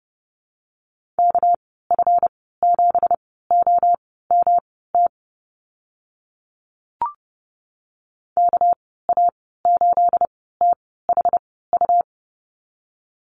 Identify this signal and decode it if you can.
Morse code, heard as a keyed tone, about 30 wpm, spaced as for 17 wpm, 700 Hz. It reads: KF7OMT KA8THU
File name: Lesson 37g - Letter x - Call Signs 30wpm